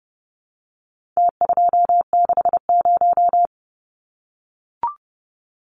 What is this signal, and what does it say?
Morse code, heard as a keyed tone, about 30 wpm, 700 Hz. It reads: T260
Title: Numbers - Contest Style - 4-Digit 30wpm